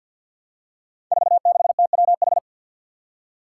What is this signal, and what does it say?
Morse code, heard as a keyed tone, about 53 wpm, 700 Hz. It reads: 46TWH
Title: Five Challenging Letter-Number Combinations 50wpm